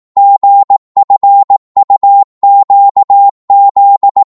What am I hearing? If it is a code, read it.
Morse code, heard as a keyed tone, about 18 wpm, 800 Hz. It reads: GFUQZ
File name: Morse / ltr-18wpm